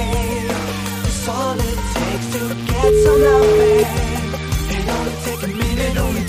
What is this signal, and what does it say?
Ringback tone (Europe, 425 Hz), with noise.